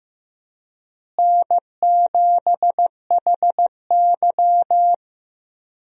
Morse code, heard as a keyed tone, about 15 wpm, 700 Hz. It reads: N7HY
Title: USA Call Signs - 4 Characters - Repeated 2x - 15wpm